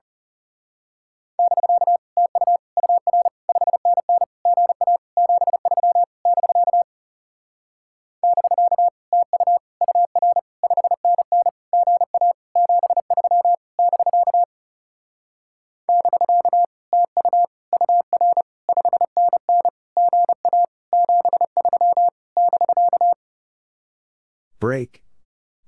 Morse code, heard as a keyed tone, about 34 wpm, 700 Hz. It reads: <BK> TU UR 5NN GA 73 <BK> <BK> TU UR 5NN GA 73 <BK> <BK> TU UR 5NN GA 73 <BK>
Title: Speed-Racing Common QSO Phrases Encore - 30wpm